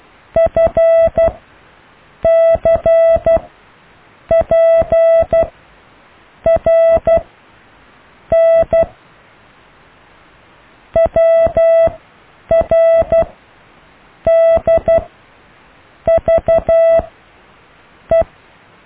Morse code, heard as a keyed tone, about 12 wpm, spaced as for 6 wpm, 660 Hz. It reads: FCPRN WRDVE
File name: cw / IJLUM+CDRO+EKTPV+BFHNW